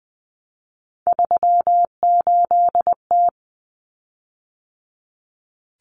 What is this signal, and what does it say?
Morse code, heard as a keyed tone, about 20 wpm, 700 Hz. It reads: 38T
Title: Common Cut Numbers - 3-Digit 20wpm